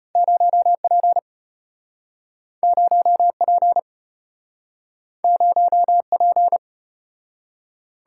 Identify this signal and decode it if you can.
Morse code, heard as a keyed tone, about 34 wpm, 700 Hz. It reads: 0P 0P 0P